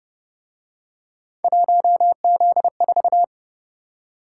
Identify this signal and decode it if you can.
Morse code, heard as a keyed tone, about 29 wpm, 700 Hz. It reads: 1Z4